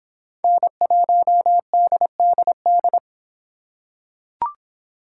Morse code, heard as a keyed tone, about 26 wpm, 700 Hz. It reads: N1DDD